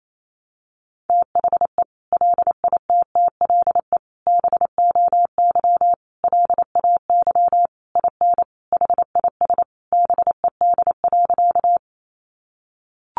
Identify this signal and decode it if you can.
Morse code, heard as a keyed tone, about 28 wpm, 700 Hz. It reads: THE LITTLE BOY LAY IN HIS BED.
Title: Sentences from Top 500 Words 28wpm